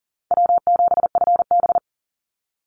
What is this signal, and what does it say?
Morse code, heard as a keyed tone, about 40 wpm, 700 Hz. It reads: W7FB
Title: Call Signs - 40wpm